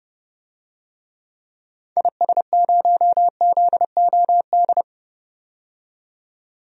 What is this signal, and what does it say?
Morse code, heard as a keyed tone, about 30 wpm, 700 Hz. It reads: IS0ZOD
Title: DXCC-100 Call Signs - 6 Characters - 30wpm